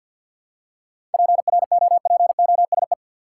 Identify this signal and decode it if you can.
Morse code, heard as a keyed tone, about 51 wpm, 700 Hz. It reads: PROPOSE
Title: Seven Letter Words - 50wpm